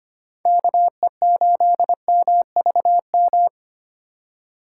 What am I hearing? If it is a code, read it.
Morse code, heard as a keyed tone, about 29 wpm, 700 Hz. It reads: KE8MVM
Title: Speed Racing - USA 2x3 Call Signs 25wpm